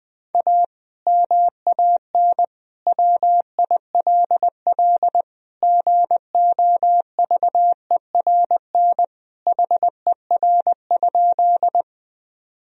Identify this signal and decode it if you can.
Morse code, heard as a keyed tone, about 20 wpm, 700 Hz. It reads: A MAN WILL GOVERN HER?
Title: Sentences from Top 1000 Words - Review Part 2 - 20wpm